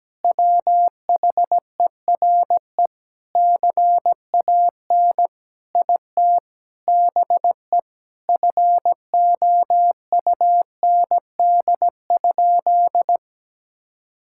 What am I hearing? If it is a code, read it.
Morse code, heard as a keyed tone, about 17 wpm, 700 Hz. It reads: WHERE CAN IT BE FOUND?
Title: Sentences from Top 1600 Words 17wpm